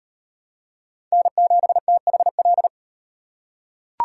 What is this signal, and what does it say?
Morse code, heard as a keyed tone, about 37 wpm, 700 Hz. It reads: N7THL